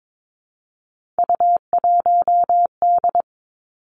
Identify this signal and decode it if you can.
Morse code, heard as a keyed tone, about 22 wpm, 700 Hz. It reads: U1D
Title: Three Challenging Letter-Number Combinations 22wpm